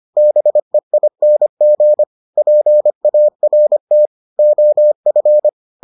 Morse code, heard as a keyed tone, about 25 wpm, 600 Hz. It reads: BEING PART OF